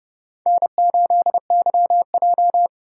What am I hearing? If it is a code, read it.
Morse code, heard as a keyed tone, about 30 wpm, 700 Hz. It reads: N8YJ
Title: USA Call Signs - 3x Character Spacing - 30wpm